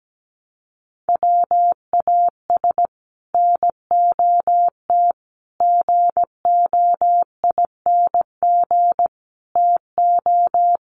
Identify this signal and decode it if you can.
Morse code, heard as a keyed tone, about 17 wpm, 700 Hz. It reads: WAS NOT GOING TO
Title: Sets of 4 Words 17wpm